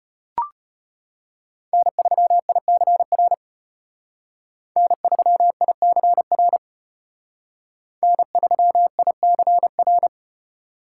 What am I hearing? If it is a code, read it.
Morse code, heard as a keyed tone, about 31 wpm, 700 Hz. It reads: N3ICR N3ICR N3ICR
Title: Speed Racing - USA 1x3 Call Signs 25wpm